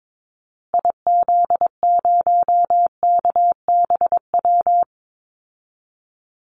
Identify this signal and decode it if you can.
Morse code, heard as a keyed tone, about 22 wpm, 700 Hz. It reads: IZ0KBW
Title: Call Signs Repeated 2x - 22wpm